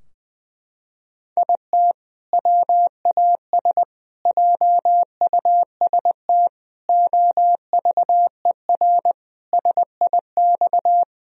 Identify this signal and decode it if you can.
Morse code, heard as a keyed tone, about 20 wpm, 700 Hz. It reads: IT WAS JUST OVER SIX
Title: Sentences from Top 1000 Words - Review Part 2 - 20wpm